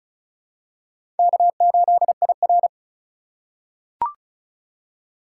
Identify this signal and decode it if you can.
Morse code, heard as a keyed tone, about 36 wpm, 700 Hz. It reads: K8IR
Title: USA Call Signs - 4 Characters - 3x Character Spacing - 35wpm